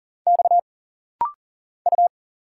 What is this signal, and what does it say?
Morse code, heard as a keyed tone, about 40 wpm, 700 Hz. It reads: X U
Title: Single Letter-Number Alt - Rapid-Fire 40wpm